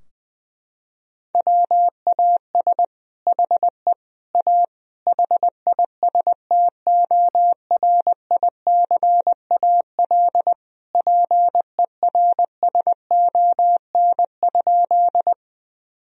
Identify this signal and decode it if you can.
Morse code, heard as a keyed tone, about 21 wpm, 700 Hz. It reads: WAS HE A HISTORICAL PERSON?
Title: Sentences from Top 1000 Words - Review Part 4 - 20wpm